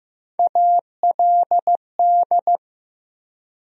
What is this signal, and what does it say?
Morse code, heard as a keyed tone, about 15 wpm, 700 Hz. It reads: ALD
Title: USA Call Sign Postfixes - 3 Letters 15wpm